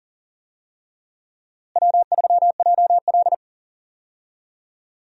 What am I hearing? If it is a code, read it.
Morse code, heard as a keyed tone, about 40 wpm, 700 Hz. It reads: W3JL
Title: USA Call Signs - 4 Characters - Repeated 3x - 40wpm